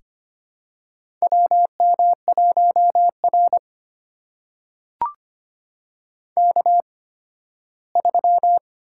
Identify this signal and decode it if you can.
Morse code, heard as a keyed tone, about 25 wpm, spaced as for 15 wpm, 700 Hz. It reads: WM1R K3
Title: USA Call Signs - 4 Characters - 8x Character Spacing - 25wpm